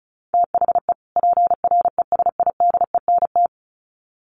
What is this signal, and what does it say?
Morse code, heard as a keyed tone, about 34 wpm, 700 Hz. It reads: THE PRESIDENT
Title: Sets of 2 Words 35wpm